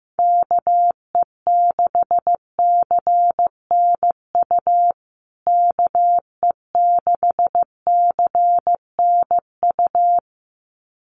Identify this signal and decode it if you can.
Morse code, heard as a keyed tone, about 15 wpm, 700 Hz. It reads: KE6CNU KE6CNU
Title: USA Call Signs Repeated 2x 15wpm